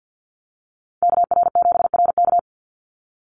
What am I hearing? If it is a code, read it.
Morse code, heard as a keyed tone, about 50 wpm, 700 Hz. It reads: KR7RK
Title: NA SOTA Chaser Callsigns 50wpm